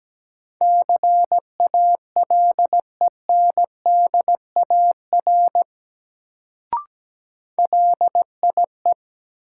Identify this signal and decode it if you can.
Morse code, heard as a keyed tone, about 17 wpm, 700 Hz. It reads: CALENDAR LIE
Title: Sentences from Top 1600 Words - New Word Drill Alt - 17wpm